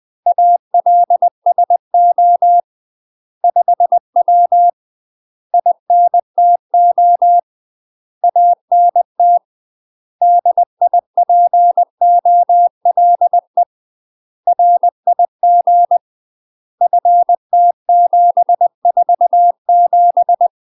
Morse code, heard as a keyed tone, about 20 wpm, 700 Hz. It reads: ALSO 5W INTO ANT DIPOLE RIG FT747
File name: UK MASTER QSOs TRAINING / 20WPM / UK MASTER QSOs 0063